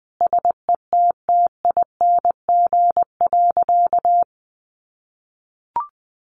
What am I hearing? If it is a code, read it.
Morse code, heard as a keyed tone, about 20 wpm, 700 Hz. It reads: SETTING.
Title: Sentences from Top 1300 Words 20wpm